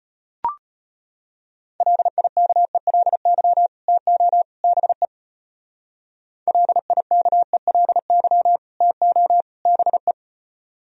Morse code, heard as a keyed tone, about 29 wpm, 700 Hz. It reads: LIKELY TO BE LIKELY TO BE
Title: Speed-Racing - Sets of 3 Words 25wpm